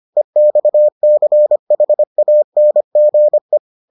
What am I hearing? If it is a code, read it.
Morse code, heard as a keyed tone, about 25 wpm, 600 Hz. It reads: EXCHANGE